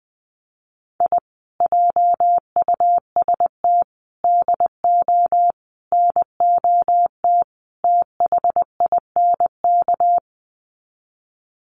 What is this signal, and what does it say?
Morse code, heard as a keyed tone, about 20 wpm, 700 Hz. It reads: I JUST DO NOT THINK
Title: Sets of 5 Words Encore 20wpm